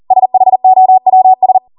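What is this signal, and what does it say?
Morse code, heard as a keyed tone, about 40 wpm, 750 Hz. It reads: SHOWS
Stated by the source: Code practice at 40 wpm.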